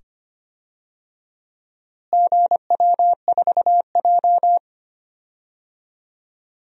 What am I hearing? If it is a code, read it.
Morse code, heard as a keyed tone, about 25 wpm, 700 Hz. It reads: GW4J
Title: DXCC-100 Call Signs - 4 Characters - Repeated 2x - 25wpm